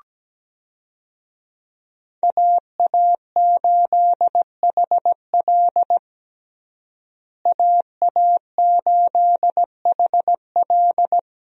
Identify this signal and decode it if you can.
Morse code, heard as a keyed tone, about 17 wpm, 700 Hz. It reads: AA8HL AA8HL
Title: USA Call Signs - 5 Characters - Repeated 3x - 17wpm